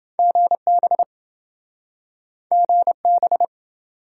Morse code, heard as a keyed tone, about 26 wpm, 700 Hz. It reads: GB GB